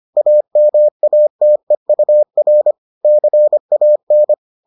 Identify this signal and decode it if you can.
Morse code, heard as a keyed tone, about 25 wpm, 600 Hz. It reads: AMATEUR CAN